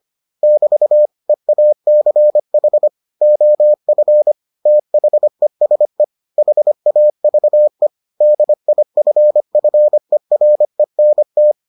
Morse code, heard as a keyed tone, about 25 wpm, 600 Hz. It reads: <BT> EACH OF THESE HAVE DIFFERENT